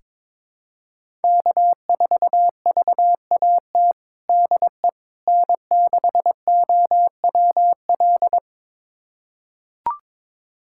Morse code, heard as a keyed tone, about 22 wpm, 700 Hz. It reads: K4VAT DE N6OWL
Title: USA Callsign DE Callsign - 5 Characters - 22wpm